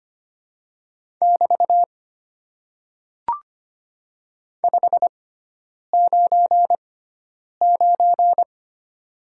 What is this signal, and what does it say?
Morse code, heard as a keyed tone, about 25 wpm, spaced as for 9 wpm, 690 Hz. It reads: <BT> 599
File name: Top 90 QSO Elements - 6x Character Spacing - 25wpm